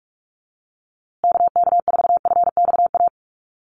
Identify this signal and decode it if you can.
Morse code, heard as a keyed tone, about 46 wpm, 700 Hz. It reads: KK4FXA
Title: USA Call Signs Repeated 2x 45wpm